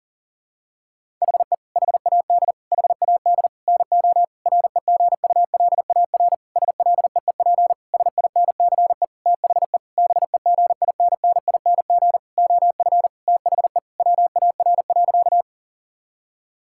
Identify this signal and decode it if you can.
Morse code, heard as a keyed tone, about 40 wpm, 700 Hz. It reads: HE HAD HAD NO REGULAR SLEEP SINCE THE BEGINNING OF THE WAR.
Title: Sentences from Top 1000 Words - Review Part 6 - 40wpm